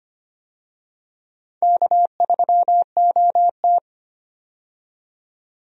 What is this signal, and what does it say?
Morse code, heard as a keyed tone, about 25 wpm, 700 Hz. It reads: K3OT
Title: USA Call Signs - 4 Characters - 25wpm